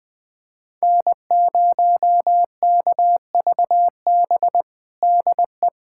Morse code, heard as a keyed tone, about 20 wpm, 700 Hz. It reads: N0KVB DE
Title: USA Callsign DE Callsign - 5 Characters - 20wpm